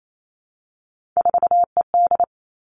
Morse code, heard as a keyed tone, about 28 wpm, 700 Hz. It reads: 4ED
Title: Three Challenging Letter-Number Combinations 28wpm